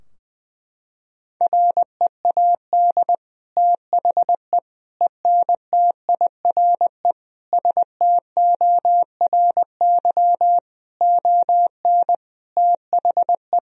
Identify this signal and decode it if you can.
Morse code, heard as a keyed tone, about 20 wpm, 700 Hz. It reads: READ THE ENTIRE STORY ON THE
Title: Sentences from Top 1000 Words - Review Part 4 - 20wpm